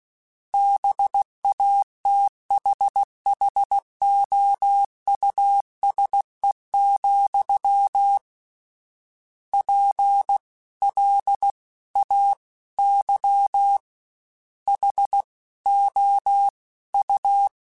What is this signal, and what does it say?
Morse code, heard as a keyed tone, about 16 wpm, 800 Hz. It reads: BATHHOUSE, P L A Y H O U
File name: General / General-mp3s / Houses @15